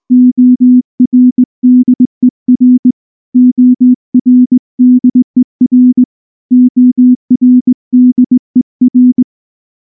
Morse code, heard as a keyed tone, about 18 wpm, 250 Hz. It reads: ORDER ORDER ORDER
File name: morse / morse-training-20WPM-10